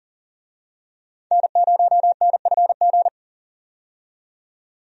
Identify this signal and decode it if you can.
Morse code, heard as a keyed tone, about 39 wpm, 700 Hz. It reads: N0NFG
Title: USA Call Signs - 5 Characters - 40wpm